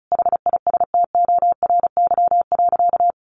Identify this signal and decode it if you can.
Morse code, heard as a keyed tone, about 34 wpm, 700 Hz. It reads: HISTORY.
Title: A Sound Of Thunder 35wpm